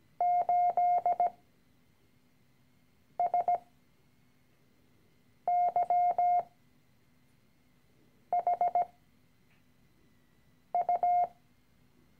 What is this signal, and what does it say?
Morse code, heard as a keyed tone, about 17 wpm, spaced as for 4 wpm, 710 Hz. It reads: 8SYHU
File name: CW-Online-Kurs / Anfänger / Lektion 013